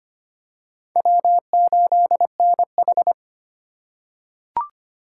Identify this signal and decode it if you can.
Morse code, heard as a keyed tone, about 25 wpm, 700 Hz. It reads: W8NH
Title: USA Call Signs - 4 Characters - 2x Character Spacing - 25wpm